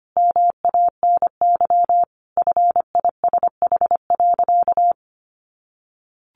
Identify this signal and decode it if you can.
Morse code, heard as a keyed tone, about 25 wpm, 700 Hz. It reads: MANY FISH.
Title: Sentences from Top 1500 Words 25wpm